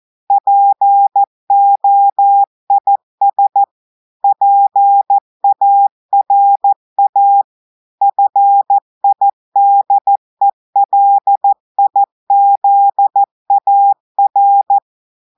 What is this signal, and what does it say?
Morse code, heard as a keyed tone, about 14 wpm, 800 Hz. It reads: POIS PARA FIDELIZAR
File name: audios / portugais / portugais texte en clair 2 vitesse 14 mots minutes